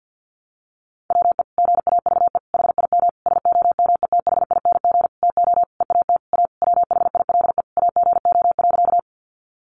Morse code, heard as a keyed tone, about 50 wpm, 700 Hz. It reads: WE GAVE HIM SOMETHING TO EAT A WHILE AGO.
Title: Sentences from Top 600 Words 50wpm